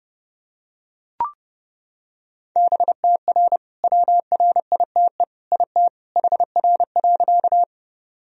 Morse code, heard as a keyed tone, about 28 wpm, 700 Hz. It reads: BTR WRITE IT HR.
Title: Sentences from Top 400 Words and Abbreviations 30wpm